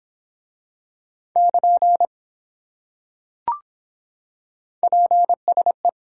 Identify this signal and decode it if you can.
Morse code, heard as a keyed tone, about 24 wpm, 700 Hz. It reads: <KN> PSE